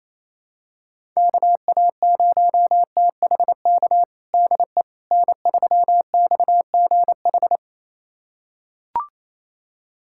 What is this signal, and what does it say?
Morse code, heard as a keyed tone, about 27 wpm, 700 Hz. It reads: KA0THK DE N3XGH
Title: USA Callsign DE Callsign 28wpm